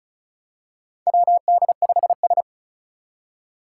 Morse code, heard as a keyed tone, about 36 wpm, 700 Hz. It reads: WD5S
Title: Four Challenging Letter-Number Combinations 35wpm